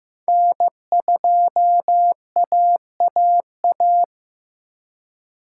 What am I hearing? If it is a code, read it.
Morse code, heard as a keyed tone, about 18 wpm, 700 Hz. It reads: N2AAA
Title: Speed Racing - USA 1x3 Call Signs 15wpm